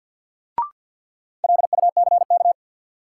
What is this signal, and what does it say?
Morse code, heard as a keyed tone, about 53 wpm, 700 Hz. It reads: LUCK
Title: Sentences from Top 1500 Words - New Word Drill Alt - 50wpm